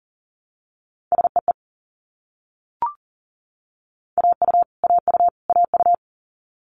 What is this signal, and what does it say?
Morse code, heard as a keyed tone, about 40 wpm, 700 Hz. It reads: SEE AU AU AU
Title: Multi-letter Phonemes Repeated 3x 40wpm